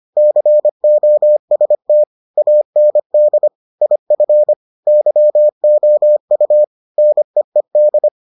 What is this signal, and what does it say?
Morse code, heard as a keyed tone, about 25 wpm, 600 Hz. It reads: COST AND IF YOU NEED